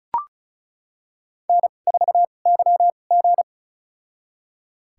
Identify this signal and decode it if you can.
Morse code, heard as a keyed tone, about 34 wpm, spaced as for 25 wpm, 700 Hz. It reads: N4YG
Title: USA Call Signs - 4 Characters - 2x Character Spacing - 35wpm